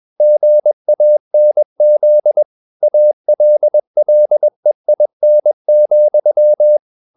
Morse code, heard as a keyed tone, about 21 wpm, 600 Hz. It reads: GANZ ALLEIN,